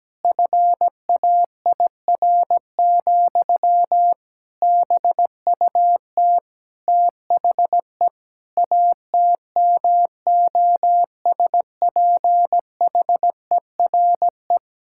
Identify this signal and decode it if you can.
Morse code, heard as a keyed tone, about 17 wpm, 700 Hz. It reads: FAIR, BUT THE ATMOSPHERE
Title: Sentences from Top 1600 Words 17wpm